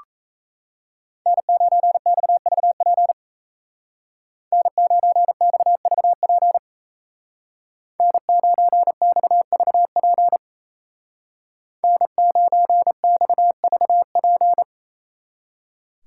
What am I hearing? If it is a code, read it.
Morse code, heard as a keyed tone, about 33 wpm, 700 Hz. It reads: N9XVP N9XVP N9XVP N9XVP
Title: Speed Racing - USA Call Signs 28wpm